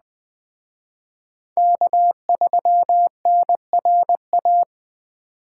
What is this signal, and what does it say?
Morse code, heard as a keyed tone, about 20 wpm, 700 Hz. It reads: K3NRA
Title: Speed Racing - USA 1x3 Call Signs 17wpm